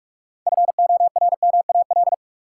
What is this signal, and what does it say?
Morse code, heard as a keyed tone, about 43 wpm, 700 Hz. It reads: FORMAL
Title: Sentences from Top 1300 Words - New Word Drill - 45wpm